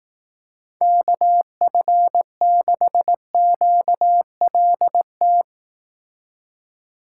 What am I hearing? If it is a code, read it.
Morse code, heard as a keyed tone, about 18 wpm, 700 Hz. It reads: KF6QLT
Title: Speed Racing - USA Call Signs 15wpm